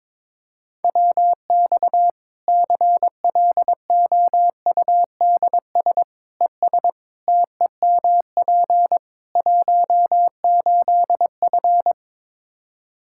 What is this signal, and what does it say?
Morse code, heard as a keyed tone, about 22 wpm, 700 Hz. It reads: WX CLOUDS ES TEMP 18F
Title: Practice QSO - Exchange 2 - Info - 22wpm